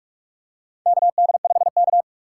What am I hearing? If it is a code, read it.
Morse code, heard as a keyed tone, about 46 wpm, 700 Hz. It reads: KD5K
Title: USA Call Signs - 4 Characters - 4x Character Spacing - 45wpm